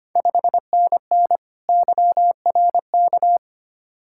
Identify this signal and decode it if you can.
Morse code, heard as a keyed tone, about 25 wpm, 700 Hz. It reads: 5NN YRK
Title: Ontario QSO Party Exchange 25wpm